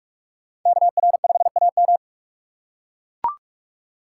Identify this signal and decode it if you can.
Morse code, heard as a keyed tone, about 41 wpm, 700 Hz. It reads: KR5AM